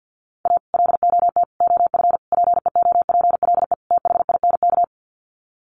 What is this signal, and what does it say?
Morse code, heard as a keyed tone, about 50 wpm, 700 Hz. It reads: A LOT OF PEOPLE THINK
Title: Sets of 5 Words Encore 50wpm